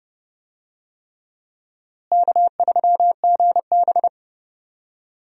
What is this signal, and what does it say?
Morse code, heard as a keyed tone, about 30 wpm, 700 Hz. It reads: K3GB
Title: USA Call Signs - 4 Characters - Repeated 3x - 30wpm